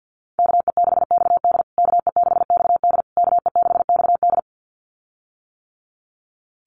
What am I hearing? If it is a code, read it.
Morse code, heard as a keyed tone, about 50 wpm, 700 Hz. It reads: KE6XD KE6XD KE6XD
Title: USA Call Signs Repeated 3x 50wpm